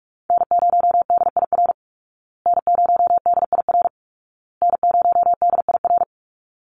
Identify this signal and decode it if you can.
Morse code, heard as a keyed tone, about 44 wpm, 700 Hz. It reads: N0DIR N0DIR N0DIR
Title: Call Signs Repeated 2x - 45wpm